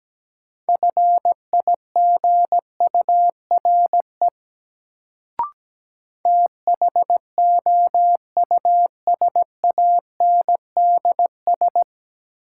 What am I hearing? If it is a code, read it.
Morse code, heard as a keyed tone, about 17 wpm, 700 Hz. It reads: FIGURE THOUSANDS
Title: Sets of 4 Words - New Word Drill - 17wpm